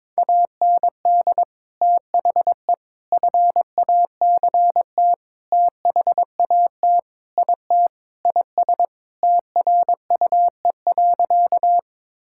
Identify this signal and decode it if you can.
Morse code, heard as a keyed tone, about 22 wpm, 700 Hz. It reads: AND THE FACT THAT IT IS TRUE.
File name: practice / 22wpm / Sentences with 4-Word Phrases - 22wpm